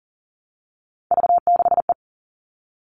Morse code, heard as a keyed tone, about 40 wpm, 700 Hz. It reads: V6E
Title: All Cut Numbers - 3-Digit 40wpm